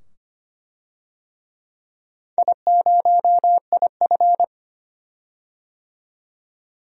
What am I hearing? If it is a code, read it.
Morse code, heard as a keyed tone, about 25 wpm, 700 Hz. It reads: I0IF
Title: DXCC-100 Call Signs - 4 Characters - 25wpm